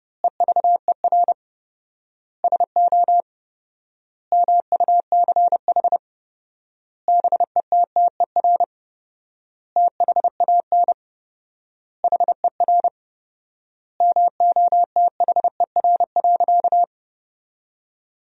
Morse code, heard as a keyed tone, about 30 wpm, 700 Hz. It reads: EVER SO MUCH BETTER THAN HER MOTHER.
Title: 4x Word-Spacing - Sentences from Top 400 Words - 30wpm